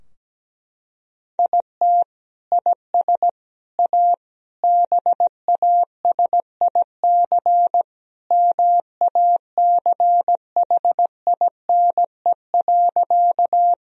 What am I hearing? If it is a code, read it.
Morse code, heard as a keyed tone, about 17 wpm, 700 Hz. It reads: IT IS A BASIC MACHINE.
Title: Sentences from Top 1000 Words - Review Part 3 - 17wpm